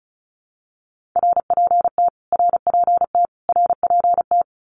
Morse code, heard as a keyed tone, about 34 wpm, 700 Hz. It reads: RPT RPT RPT
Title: Abbreviations Repeated 3x - 35wpm